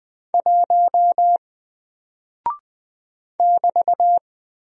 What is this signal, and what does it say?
Morse code, heard as a keyed tone, about 20 wpm, 700 Hz. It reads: 1 <BT>
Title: Single Character (Letter:Number:Punctuation:Prosign) Alt 20wpm